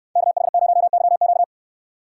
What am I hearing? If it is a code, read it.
Morse code, heard as a keyed tone, about 68 wpm, 700 Hz. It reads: KH0YQ